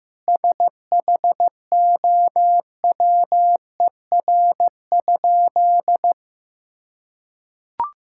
Morse code, heard as a keyed tone, about 15 wpm, 700 Hz. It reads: SHOWER?
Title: Sentences from Top 1600 Words 15wpm